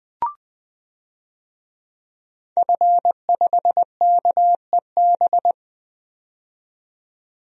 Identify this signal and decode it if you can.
Morse code, heard as a keyed tone, about 20 wpm, 700 Hz. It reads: F5KEB
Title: DXCC-100 Call Signs - 5 Characters - 20wpm